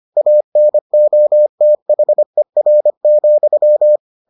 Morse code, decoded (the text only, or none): ANOTHER,